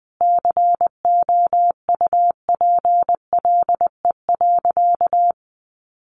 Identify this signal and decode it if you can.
Morse code, heard as a keyed tone, about 20 wpm, 700 Hz. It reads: COUPLE.
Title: Sentences from Top 1100 Words 20wpm